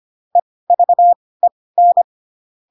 Morse code, heard as a keyed tone, about 25 wpm, spaced as for 18 wpm, 700 Hz. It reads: EVEN
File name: Pinocchio / Pinocchio ch6 12 25 ik1hge